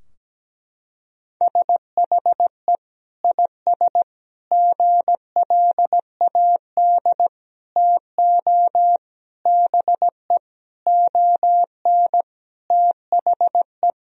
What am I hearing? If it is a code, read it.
Morse code, heard as a keyed tone, about 17 wpm, 700 Hz. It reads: SHE IS GLAD TO BE ON THE
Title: Sentences from Top 1000 Words - Review Part 2 - 17wpm